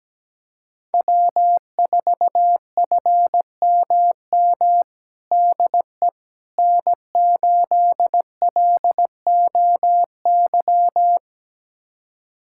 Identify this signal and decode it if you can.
Morse code, heard as a keyed tone, about 17 wpm, 700 Hz. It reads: W4FMM DE N8LOY
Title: USA Callsign DE Callsign - 5 Characters - 17wpm